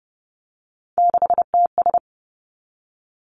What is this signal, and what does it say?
Morse code, heard as a keyed tone, about 30 wpm, 700 Hz. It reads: 6TS
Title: Three Challenging Letter-Number Combinations 30wpm